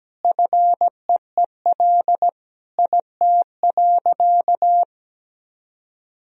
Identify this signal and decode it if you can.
Morse code, heard as a keyed tone, about 17 wpm, 700 Hz. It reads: FEEL IT.
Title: Sentences from Top 400 Words and Abbreviations 17wpm